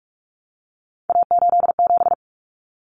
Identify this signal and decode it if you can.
Morse code, heard as a keyed tone, about 44 wpm, 700 Hz. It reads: A87
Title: Common Cut Numbers - 3-Digit 45wpm